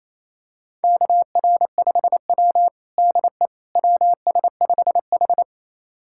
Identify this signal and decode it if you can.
Morse code, heard as a keyed tone, about 28 wpm, 700 Hz. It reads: KR5W DE WS5H
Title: USA Callsign DE Callsign - 4 Characters - 28wpm